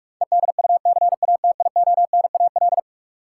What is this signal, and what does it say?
Morse code, heard as a keyed tone, about 46 wpm, 700 Hz. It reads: EDUCATIONAL
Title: Sentences from Top 1200 Words - New Word Drill Alt - 45wpm